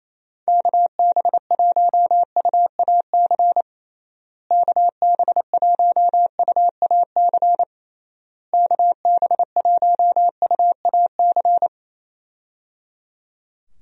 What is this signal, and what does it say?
Morse code, heard as a keyed tone, about 28 wpm, 700 Hz. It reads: KB1UAC KB1UAC KB1UAC
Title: USA Call Signs - 6 Characters - Repeated 3x - 28wpm